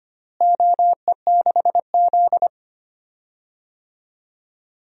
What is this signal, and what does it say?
Morse code, heard as a keyed tone, about 25 wpm, 700 Hz. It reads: OE6Z
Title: DXCC-100 Call Signs - 4 Characters - Repeated 2x - 25wpm